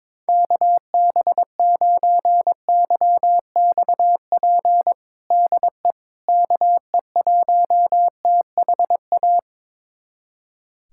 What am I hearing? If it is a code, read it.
Morse code, heard as a keyed tone, about 22 wpm, 700 Hz. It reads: KB9YXP DE KE1THA
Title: USA Callsign DE Callsign 22wpm